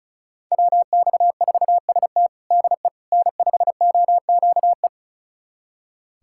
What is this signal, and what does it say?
Morse code, heard as a keyed tone, about 36 wpm, 700 Hz. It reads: WX4ST DE N5OQE
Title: USA Callsign DE Callsign 35wpm